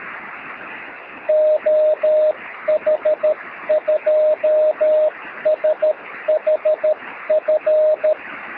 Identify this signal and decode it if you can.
Morse code, heard as a keyed tone, about 13 wpm, 620 Hz, with radio noise. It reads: OH2SHF
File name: sm7lcb / oh2shf 13cm